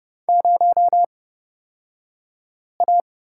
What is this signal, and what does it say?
Morse code, heard as a keyed tone, about 30 wpm, spaced as for 5 wpm, 700 Hz. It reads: 0A
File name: Lesson 37d - Letter x - Two Characters 30wpm